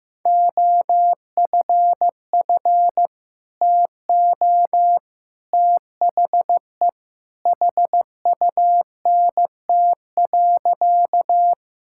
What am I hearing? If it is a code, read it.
Morse code, heard as a keyed tone, about 15 wpm, 700 Hz. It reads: OFF TO THE HUNT.
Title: Sentences from Top 1600 Words 15wpm